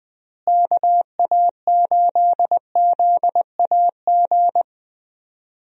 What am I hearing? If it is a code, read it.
Morse code, heard as a keyed tone, about 20 wpm, 700 Hz. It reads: KA8ZAG
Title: USA Call Signs - 6 Characters - Repeated 2x - 20wpm